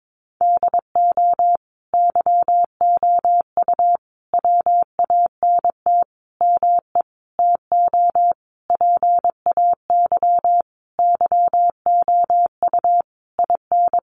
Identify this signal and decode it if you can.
Morse code, heard as a keyed tone, about 22 wpm, 700 Hz. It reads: DO YOU WANT ME TO PAY YOU IN
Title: Sentences from Top 1400 Words 22wpm